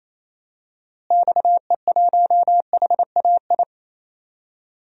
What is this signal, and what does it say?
Morse code, heard as a keyed tone, about 28 wpm, 700 Hz. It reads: XE1HAI
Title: DXCC-100 Call Signs - 6 Characters - Repeated 3x - 28wpm